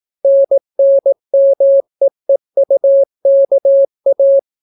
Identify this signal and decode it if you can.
Morse code, heard as a keyed tone, about 18 wpm, 550 Hz. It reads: NNMEEUKA